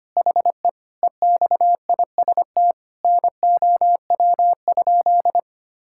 Morse code, heard as a keyed tone, about 25 wpm, 700 Hz. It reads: HE EXIST NOW?
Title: Sentences from Top 1000 Words - Review Part 5 - 25wpm